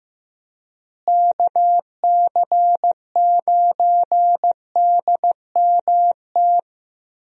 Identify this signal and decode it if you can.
Morse code, heard as a keyed tone, about 15 wpm, 700 Hz. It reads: KC9DMT
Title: USA Call Signs - 6 Characters - Repeated 2x - 15wpm